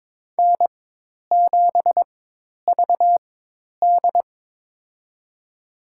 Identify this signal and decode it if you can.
Morse code, heard as a keyed tone, about 22 wpm, spaced as for 10 wpm, 700 Hz. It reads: N7VD
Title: USA Call Signs - 4 Characters - 4x Character Spacing - 22wpm